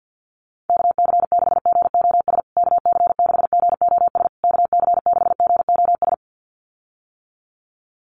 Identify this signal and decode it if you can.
Morse code, heard as a keyed tone, about 50 wpm, 700 Hz. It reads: KC6GOS KC6GOS KC6GOS
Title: USA Call Signs Repeated 3x 50wpm